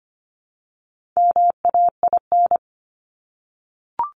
Morse code, heard as a keyed tone, about 25 wpm, 700 Hz. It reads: MAIN
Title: Top 300 Words 25wpm